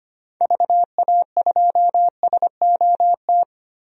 Morse code, heard as a keyed tone, about 25 wpm, 700 Hz. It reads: VA2SOT
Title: DXCC-100 Call Signs - 6 Characters - 25wpm